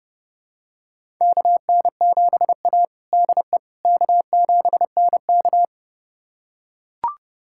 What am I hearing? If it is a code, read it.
Morse code, heard as a keyed tone, about 30 wpm, 700 Hz. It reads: KN7A DE K7NK
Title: USA Callsign DE Callsign - 4 Characters - 30wpm